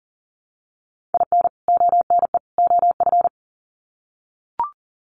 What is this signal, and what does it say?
Morse code, heard as a keyed tone, about 40 wpm, 700 Hz. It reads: IN ONE OF
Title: Sets of 3 Words 40wpm